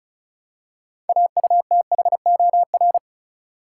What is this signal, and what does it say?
Morse code, heard as a keyed tone, about 36 wpm, 700 Hz. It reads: AUTHOR
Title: Six Letter Words - 35wpm